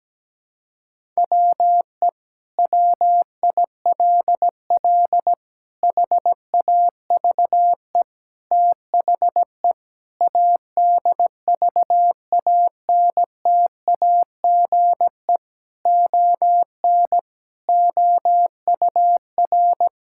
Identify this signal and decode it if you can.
Morse code, heard as a keyed tone, about 17 wpm, 700 Hz. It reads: WE WILL HAVE THE ADVANTAGE ON OUR
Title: Sentences from Top 1000 Words - Review Part 6 - 17wpm